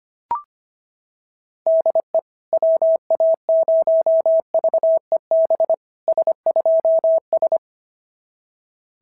Morse code, heard as a keyed tone, about 25 wpm, 690 Hz. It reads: DE WA0VEB S2S
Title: SOTA QSO Practice 25wpm